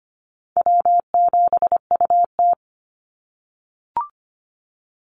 Morse code, heard as a keyed tone, about 25 wpm, 700 Hz. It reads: W7UT
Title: Easy Call Signs - 25wpm